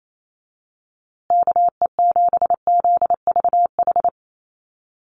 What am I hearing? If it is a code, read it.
Morse code, heard as a keyed tone, about 28 wpm, 700 Hz. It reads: KE7ZVH
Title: USA Call Signs Repeated 3x 28wpm